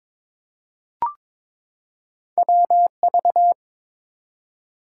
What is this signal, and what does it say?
Morse code, heard as a keyed tone, about 22 wpm, 700 Hz. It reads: WV
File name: ARRL Sections v2 - 22wpm